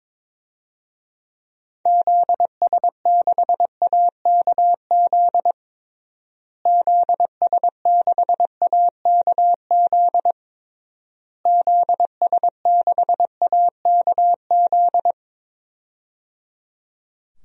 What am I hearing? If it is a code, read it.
Morse code, heard as a keyed tone, about 22 wpm, 700 Hz. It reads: ZS6AKZ ZS6AKZ ZS6AKZ
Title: DXCC-100 Call Signs - 6 Characters - Repeated 3x - 22wpm